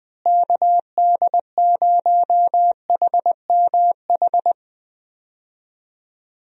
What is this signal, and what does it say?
Morse code, heard as a keyed tone, about 20 wpm, 700 Hz. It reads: KD0HMH
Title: USA Call Signs - 6 Characters - Repeated 2x - 20wpm